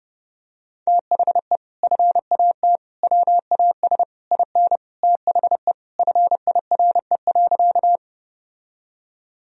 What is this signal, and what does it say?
Morse code, heard as a keyed tone, about 30 wpm, 700 Hz. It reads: THE FAT WAS IN THE FIRE.
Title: Sentences from Top 1000 Words - Review Part 4 - 30wpm